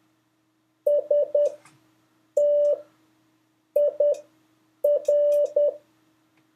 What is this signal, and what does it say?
Morse code, heard as a keyed tone, about 10 wpm, spaced as for 6 wpm, 580 Hz. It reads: STIR
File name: 3of5-words-2